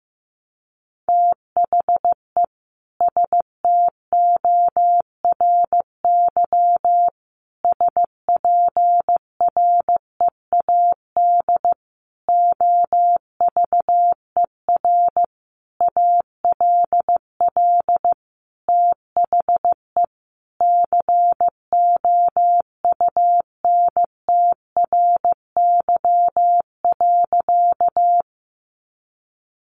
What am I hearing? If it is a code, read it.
Morse code, heard as a keyed tone, about 15 wpm, 700 Hz. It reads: THE STORY SPREAD OVER ALL THE COUNTRY.
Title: Sentences from Top 1200 Words 15wpm